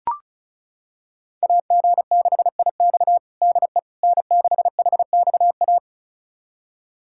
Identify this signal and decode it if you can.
Morse code, heard as a keyed tone, about 36 wpm, 700 Hz. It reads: AG6IX DE N6HXA
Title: USA Callsign DE Callsign - 5 Characters - 35wpm